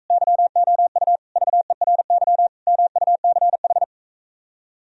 Morse code, heard as a keyed tone, about 45 wpm, 700 Hz. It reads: YOU VERY MUCH